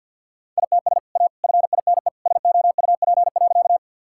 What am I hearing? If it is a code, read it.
Morse code, heard as a keyed tone, about 52 wpm, 700 Hz. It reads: ITS A FINE SOUP.
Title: Sentences from Top 1000 Words - Review Part 6 - 50wpm